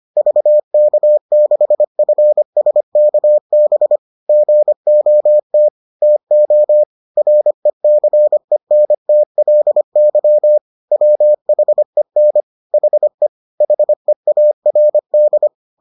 Morse code, heard as a keyed tone, about 25 wpm, 600 Hz. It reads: VK6FSKB GOT TO RECENTLY WHEN HE HEARD